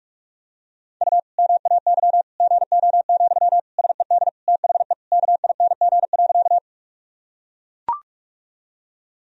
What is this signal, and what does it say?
Morse code, heard as a keyed tone, about 42 wpm, 700 Hz. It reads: U MAY GO, SED THE KING.
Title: Sentences from Top 400 Words and Abbreviations 45wpm